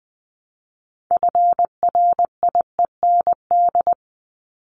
Morse code, heard as a keyed tone, about 20 wpm, 700 Hz. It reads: FRIEND
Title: Top 500 Words 20wpm